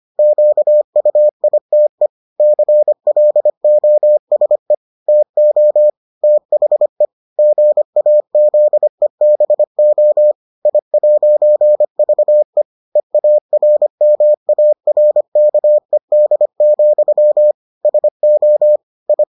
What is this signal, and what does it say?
Morse code, heard as a keyed tone, about 25 wpm, 600 Hz. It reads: QUITE CLOSE TO THE GAZEBO I'VE EARMARKED, SO I